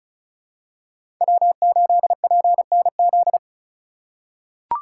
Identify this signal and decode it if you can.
Morse code, heard as a keyed tone, about 34 wpm, 700 Hz. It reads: W8PNZ
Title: USA Call Signs - 2x Character Spacing - 35wpm